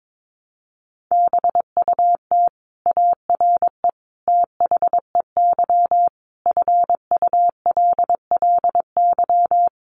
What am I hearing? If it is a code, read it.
Morse code, heard as a keyed tone, about 22 wpm, 700 Hz. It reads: BUT ARE THEY FULLY
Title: Sentences from Top 1300 Words 22wpm